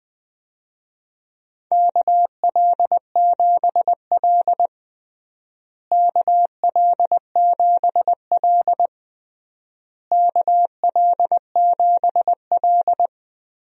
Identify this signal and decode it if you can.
Morse code, heard as a keyed tone, about 21 wpm, 700 Hz. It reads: KL7L KL7L KL7L
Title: DXCC-100 Call Signs - 4 Characters - Repeated 3x - 20wpm